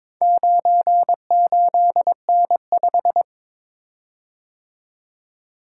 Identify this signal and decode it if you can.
Morse code, heard as a keyed tone, about 22 wpm, 690 Hz. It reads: 98N5
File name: practice / 22wpm / Four Letter-Number Combinations - 22wpm